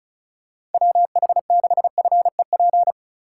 Morse code, heard as a keyed tone, about 34 wpm, 690 Hz. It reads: WH6FEP
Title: USA Call Signs - 4x Character Spacing - 35wpm